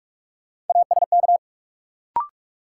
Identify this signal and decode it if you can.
Morse code, heard as a keyed tone, about 42 wpm, 700 Hz. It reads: ASK
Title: Sentences from Top 200 Words - New Word Drill - 45wpm